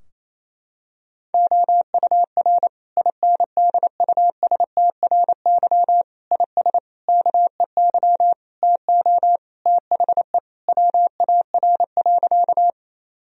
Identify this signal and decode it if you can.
Morse code, heard as a keyed tone, about 28 wpm, 700 Hz. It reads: OUR INDUSTRY IS KEY TO THE WAR.
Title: Sentences from Top 1000 Words - Review Part 4 - 28wpm